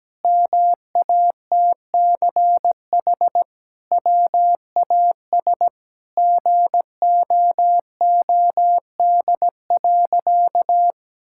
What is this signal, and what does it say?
Morse code, heard as a keyed tone, about 17 wpm, 700 Hz. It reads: MATCH WAS GOOD.
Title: Sentences from Top 1600 Words 17wpm